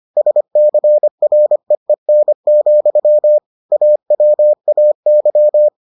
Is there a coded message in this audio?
Yes: Morse code from 0.2 to 5.7 s, heard as a keyed tone, about 25 wpm, 600 Hz. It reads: SCREEN, AWAY